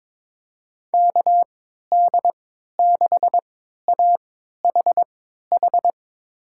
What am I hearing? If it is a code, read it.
Morse code, heard as a keyed tone, about 22 wpm, spaced as for 13 wpm, 700 Hz. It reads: KD6AHH
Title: USA Call Signs - 6 Characters - 3x Character Spacing - 22wpm